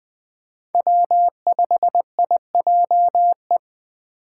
Morse code, heard as a keyed tone, about 20 wpm, 700 Hz. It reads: W5IJE